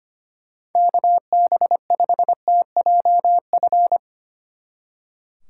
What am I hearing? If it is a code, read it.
Morse code, heard as a keyed tone, about 31 wpm, 700 Hz. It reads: KB5TJF